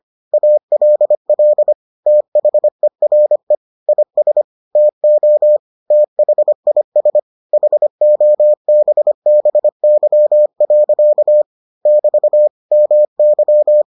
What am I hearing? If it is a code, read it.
Morse code, heard as a keyed tone, about 25 wpm, 600 Hz. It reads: ALL THERE IS TO THIS HOBBY. <BT> MY